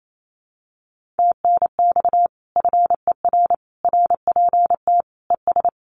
Morse code, heard as a keyed tone, about 28 wpm, 700 Hz. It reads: TNX FER RPT ES
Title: Common QSO Phrases Encore 28wpm